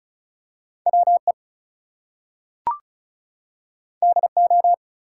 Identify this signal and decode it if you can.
Morse code, heard as a keyed tone, about 36 wpm, 700 Hz. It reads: WE DO
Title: Two Letter Words - 35wpm